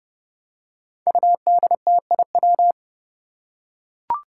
Morse code, heard as a keyed tone, about 30 wpm, 700 Hz. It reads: UDTIW
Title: Five Challenging Letter-Number Combinations 30wpm